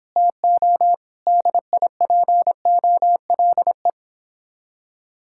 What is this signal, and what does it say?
Morse code, heard as a keyed tone, about 26 wpm, 700 Hz. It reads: TO DIPOLE
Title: Speed-Racing Common QSO Phrases Encore - 22wpm